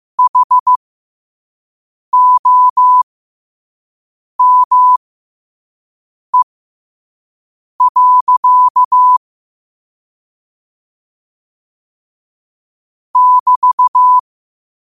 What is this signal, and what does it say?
Morse code, heard as a keyed tone, about 15 wpm, spaced as for 5 wpm, 1000 Hz. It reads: HOME. <BT>
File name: qotdfiles / 5wpm